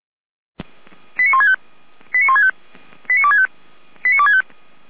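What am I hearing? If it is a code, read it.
Five-tone selective call (ZVEI), heard as quick runs of five tones, tones of about 70 ms. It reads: 89165 89165 89265 89265
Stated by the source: ZVEI five-tone selective call received on SDR.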